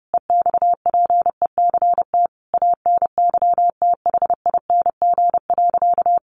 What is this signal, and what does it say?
Morse code, heard as a keyed tone, about 30 wpm, 700 Hz. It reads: EXPECT ANYTHING.
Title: Sentences from Top 1400 Words 30wpm